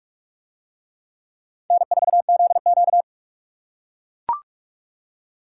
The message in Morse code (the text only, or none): N4ZQ